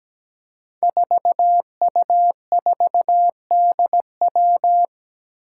Morse code, heard as a keyed tone, about 17 wpm, 700 Hz. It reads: 4U4DW
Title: Five Challenging Letter-Number Combinations 17wpm